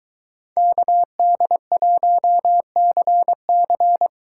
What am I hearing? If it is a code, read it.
Morse code, heard as a keyed tone, about 21 wpm, 700 Hz. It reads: KD1CC